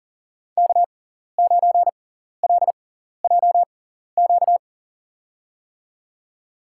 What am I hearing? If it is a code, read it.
Morse code, heard as a keyed tone, about 40 wpm, spaced as for 14 wpm, 700 Hz. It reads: K9LJQ
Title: USA Call Signs - 6x Character Spacing - 40wpm